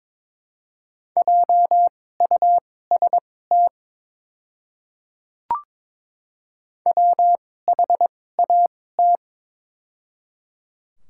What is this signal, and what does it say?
Morse code, heard as a keyed tone, about 23 wpm, spaced as for 16 wpm, 700 Hz. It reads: JUST WHAT
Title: 2x Character Spacing - Top 200 Words - 22wpm